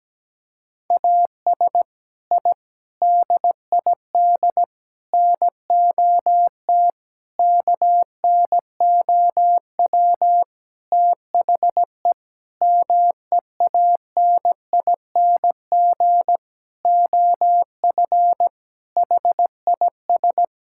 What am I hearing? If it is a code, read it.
Morse code, heard as a keyed tone, about 17 wpm, 700 Hz. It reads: AS I DID NOT KNOW THE MEANING OF HIS
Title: Sentences from Top 1000 Words - Review Part 6 - 17wpm